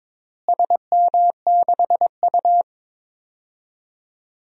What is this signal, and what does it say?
Morse code, heard as a keyed tone, about 22 wpm, 700 Hz. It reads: SM6U
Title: DXCC-100 Call Signs - 4 Characters - Repeated 2x - 22wpm